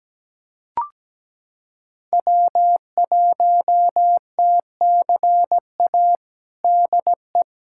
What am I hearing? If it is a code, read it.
Morse code, heard as a keyed tone, about 17 wpm, 700 Hz. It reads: W1TCA DE
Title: USA Callsign DE Callsign 17wpm